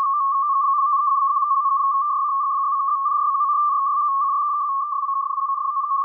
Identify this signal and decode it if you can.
Two-tone teleprinter signal, about 80 Hz shift.